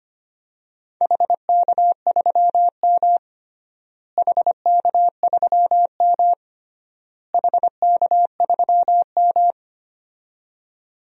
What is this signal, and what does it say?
Morse code, heard as a keyed tone, about 25 wpm, 700 Hz. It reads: HK3M HK3M HK3M
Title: DXCC-100 Call Signs - 4 Characters - Repeated 3x - 25wpm